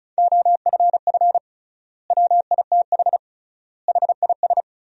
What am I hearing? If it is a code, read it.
Morse code, heard as a keyed tone, about 36 wpm, 700 Hz. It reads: OFF WITH HIS
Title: 3x Word-Spacing - Sentences from Top 300 Words - 35wpm